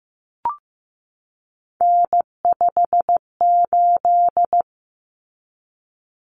Morse code, heard as a keyed tone, about 15 wpm, 700 Hz. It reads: N58
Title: Common Cut Numbers - 3-Digit 15wpm